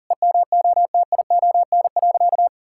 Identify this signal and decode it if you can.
Morse code, heard as a keyed tone, about 40 wpm, 700 Hz. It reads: EMOTION.
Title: Sentences from Top 1600 Words 40wpm